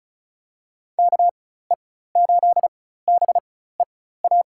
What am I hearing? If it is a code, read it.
Morse code, heard as a keyed tone, about 34 wpm, spaced as for 16 wpm, 700 Hz. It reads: KE8BEA
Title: USA Call Signs - 6 Characters - 4x Character Spacing - 35wpm